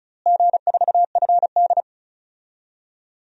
Morse code, heard as a keyed tone, about 36 wpm, 700 Hz. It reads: G4FD
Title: DXCC-100 Call Signs - 4 Characters - Repeated 3x - 35wpm